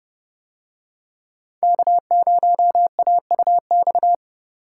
Morse code, heard as a keyed tone, about 28 wpm, 700 Hz. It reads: K0AUX